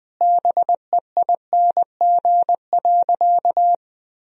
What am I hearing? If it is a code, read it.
Morse code, heard as a keyed tone, about 20 wpm, 700 Hz. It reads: BEING.
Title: Sentences from Top 1000 Words - Review Part 4 - 20wpm